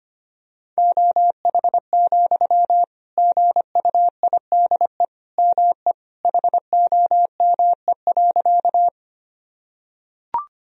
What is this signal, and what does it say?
Morse code, heard as a keyed tone, about 25 wpm, 700 Hz. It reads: OH, GUIDE ME HOME.
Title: Sentences from Top 1000 Words - Review Part 3 - 25wpm